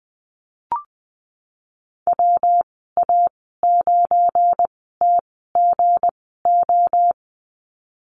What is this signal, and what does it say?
Morse code, heard as a keyed tone, about 20 wpm, spaced as for 15 wpm, 700 Hz. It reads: WA9TGO
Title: 193 Lesson 31g - Letter v - Call Signs 20wpm